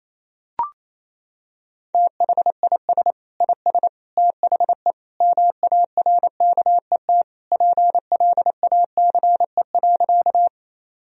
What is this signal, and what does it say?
Morse code, heard as a keyed tone, about 28 wpm, 700 Hz. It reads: THIS IS THE MARKET PLACE.
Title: Sentences from Top 1000 Words - Review Part 3 - 28wpm